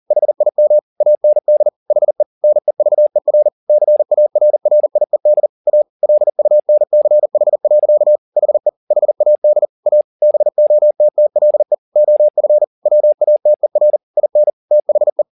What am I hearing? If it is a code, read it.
Morse code, heard as a keyed tone, about 40 wpm, 600 Hz. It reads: HIM AND HE NEVER CARRIED A LUNCH. HE HAD A BOTTLE OF WATER IN THE